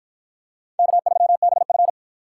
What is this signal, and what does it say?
Morse code, heard as a keyed tone, about 53 wpm, 700 Hz. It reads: K3BF